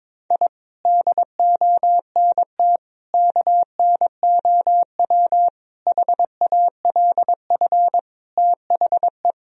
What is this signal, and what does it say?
Morse code, heard as a keyed tone, about 22 wpm, 700 Hz. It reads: I DONT KNOW HALF THE
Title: Sentences from Top 1000 Words - Review Part 1 - 22wpm